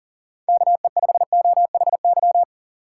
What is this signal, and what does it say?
Morse code, heard as a keyed tone, about 37 wpm, 700 Hz. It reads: KE5OHY
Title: USA Call Signs - 6 Characters - 3x Character Spacing - 40wpm